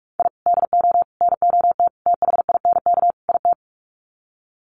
Morse code, heard as a keyed tone, about 44 wpm, 700 Hz. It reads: I DO NOT THINK IT
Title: Sets of 5 Words 45wpm